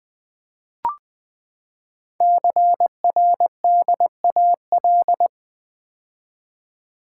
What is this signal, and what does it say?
Morse code, heard as a keyed tone, about 20 wpm, 700 Hz. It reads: CRDAL
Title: Code Groups - 5 Letters - 20wpm